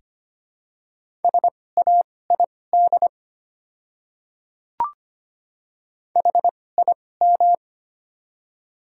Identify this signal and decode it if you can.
Morse code, heard as a keyed tone, about 25 wpm, spaced as for 18 wpm, 700 Hz. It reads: SAID HIM
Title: 2x Character Spacing - Top 100 Words - 25wpm